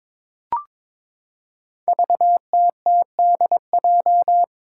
Morse code, heard as a keyed tone, about 22 wpm, 700 Hz. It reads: VTTDJ
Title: Five Challenging Letter-Number Combinations 22wpm